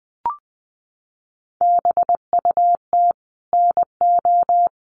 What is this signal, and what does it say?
Morse code, heard as a keyed tone, about 20 wpm, 700 Hz. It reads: BUT NO
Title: Sentences from Top 1500 Words 20wpm